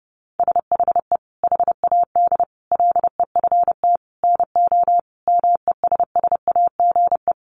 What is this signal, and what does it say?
Morse code, heard as a keyed tone, about 30 wpm, 700 Hz. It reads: SHE HAD LEFT NO MESSAGE
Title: Sentences from Top 1100 Words 30wpm